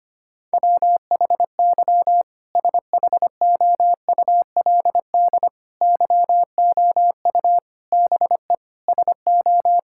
Morse code, heard as a keyed tone, about 25 wpm, 700 Hz. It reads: WHY SHOULD YOU BE SO
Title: Sentences from Top 1000 Words - Review Part 6 - 25wpm